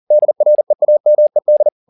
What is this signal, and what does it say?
Morse code, heard as a keyed tone, about 40 wpm, 600 Hz. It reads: DREAMED